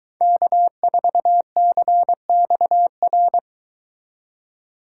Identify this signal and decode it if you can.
Morse code, heard as a keyed tone, about 22 wpm, 700 Hz. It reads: K4CXR